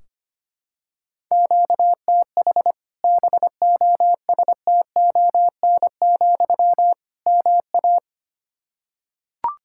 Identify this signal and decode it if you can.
Morse code, heard as a keyed tone, about 25 wpm, 700 Hz. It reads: QTH BOSTON, MA
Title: Practice QSO - Exchange 1 - Beginning - 25wpm